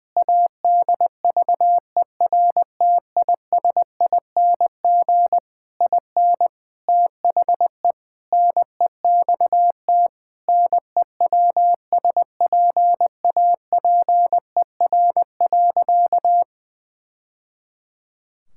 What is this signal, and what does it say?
Morse code, heard as a keyed tone, about 20 wpm, 700 Hz. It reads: ADVERTISING IN THE NEXT NEWSPAPER.
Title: Sentences from Top 1000 Words - Review Part 6 - 20wpm